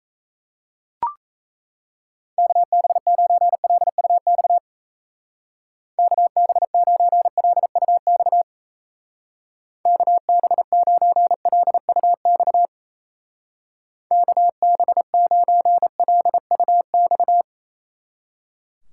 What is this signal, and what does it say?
Morse code, heard as a keyed tone, about 35 wpm, 700 Hz. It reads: KB9LUX KB9LUX KB9LUX KB9LUX
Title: Speed Racing - USA 2x3 Call Signs 28wpm